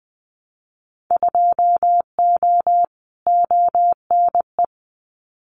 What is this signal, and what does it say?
Morse code, heard as a keyed tone, about 19 wpm, 700 Hz. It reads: 2O ONE
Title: Winter Field Day Exchanges Encore 20wpm